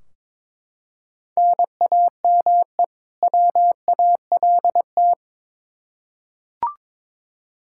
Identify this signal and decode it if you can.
Morse code, heard as a keyed tone, about 22 wpm, 700 Hz. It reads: NAME WALT
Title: Practice QSO - Exchange 1 - Beginning - 22wpm